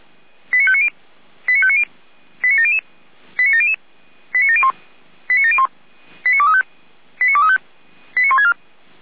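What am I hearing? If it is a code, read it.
Five-tone selective call (ZVEI), heard as quick runs of five tones, tones of about 70 ms. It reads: 89590 89590 89700 89700 89701 89701 89235 89235 89164
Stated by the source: ZVEI five-tone call received by SDR.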